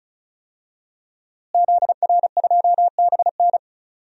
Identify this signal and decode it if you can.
Morse code, heard as a keyed tone, about 34 wpm, 700 Hz. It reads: ZR2BN
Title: DXCC-100 Call Signs - 5 Characters - Repeated 3x - 35wpm